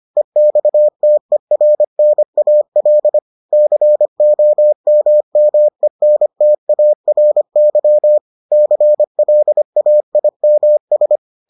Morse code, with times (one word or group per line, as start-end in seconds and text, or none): 0.2-3.2 EXTERNAL
3.5-8.2 COMMENTARY
8.5-11.2 CLAIMS